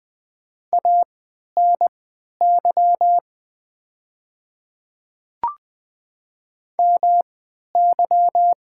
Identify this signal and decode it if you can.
Morse code, heard as a keyed tone, about 20 wpm, spaced as for 11 wpm, 700 Hz. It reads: ANY MY
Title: 3x Character Spacing - Top 200 Words - 20wpm